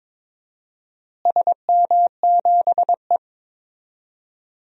Morse code, heard as a keyed tone, about 21 wpm, 700 Hz. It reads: SM7E
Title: DXCC-100 Call Signs - 4 Characters - Repeated 3x - 22wpm